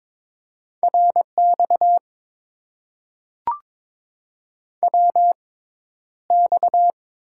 Morse code, heard as a keyed tone, about 22 wpm, spaced as for 11 wpm, 700 Hz. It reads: RX WX
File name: Top 90 QSO Elements - 6x Character Spacing - 22wpm